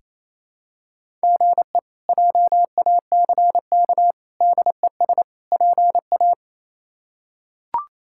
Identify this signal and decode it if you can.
Morse code, heard as a keyed tone, about 28 wpm, 700 Hz. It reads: GE JACK DES PA
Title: K1USN SST QSO Practice - 28wpm